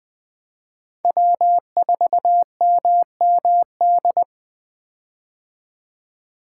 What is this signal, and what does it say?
Morse code, heard as a keyed tone, about 21 wpm, 700 Hz. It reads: W4MMD
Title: K1USN SST QSO Practice - 20wpm